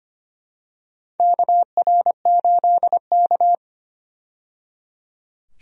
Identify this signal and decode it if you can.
Morse code, heard as a keyed tone, about 25 wpm, 700 Hz. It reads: KR8K
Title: USA Call Signs - 4 Characters - 25wpm